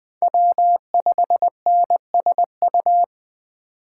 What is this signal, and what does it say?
Morse code, heard as a keyed tone, about 20 wpm, 700 Hz. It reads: W5NSU
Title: Speed Racing - USA 1x3 Call Signs 17wpm